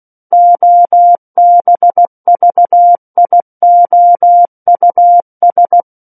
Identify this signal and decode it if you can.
Morse code, heard as a keyed tone, about 16 wpm, 700 Hz. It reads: OBVIOUS
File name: Book One / 07-Chapter 7 - 16 WPM